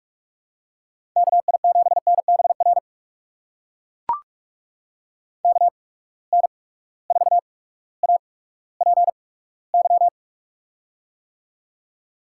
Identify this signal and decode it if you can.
Morse code, heard as a keyed tone, about 43 wpm, spaced as for 19 wpm, 700 Hz. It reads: KI7NBR KN4APY
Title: USA Call Signs - 6 Characters - 8x Character Spacing - 45wpm